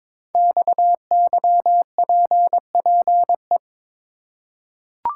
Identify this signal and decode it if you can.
Morse code, heard as a keyed tone, about 23 wpm, 700 Hz. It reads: XYPPE
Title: Code Groups - 5 Letters - 22wpm